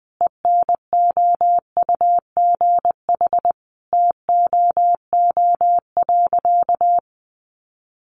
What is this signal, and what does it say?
Morse code, heard as a keyed tone, about 20 wpm, 700 Hz. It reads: ENOUGH TOO.
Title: Additional Easy Phrases 20wpm